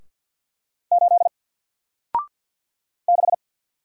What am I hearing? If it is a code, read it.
Morse code, heard as a keyed tone, about 48 wpm, 700 Hz. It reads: 8 6
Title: Single Character (Letter:Number:Punctuation:Prosign) 50wpm